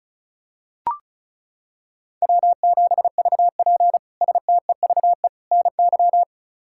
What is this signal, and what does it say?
Morse code, heard as a keyed tone, about 36 wpm, 700 Hz. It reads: W7VP STEVE NY
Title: K1USN SST QSO Practice - 35wpm